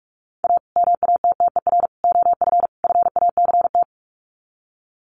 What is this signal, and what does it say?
Morse code, heard as a keyed tone, about 44 wpm, 700 Hz. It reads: A MATTER OF FACT
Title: Sets of 4 Words 45wpm